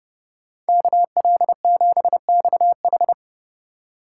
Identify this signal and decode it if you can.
Morse code, heard as a keyed tone, about 31 wpm, 700 Hz. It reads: KL7XH